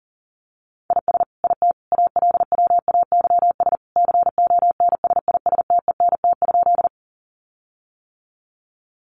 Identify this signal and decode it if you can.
Morse code, heard as a keyed tone, about 40 wpm, 700 Hz. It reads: IS IT ALWAYS CONSISTENT?
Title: Sentences from Top 1500 Words 40wpm